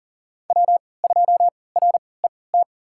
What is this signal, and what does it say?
Morse code, heard as a keyed tone, about 40 wpm, spaced as for 23 wpm, 700 Hz. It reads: W2RET
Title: USA Call Signs - 5 Characters - 3x Character Spacing - 40wpm